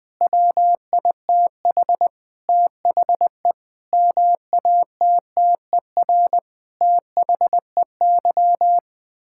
Morse code, heard as a keyed tone, about 20 wpm, 700 Hz. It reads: WITH THE MATTER THEY
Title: Sentences with 7-Word Phrases - Spoken Beginning - 20wpm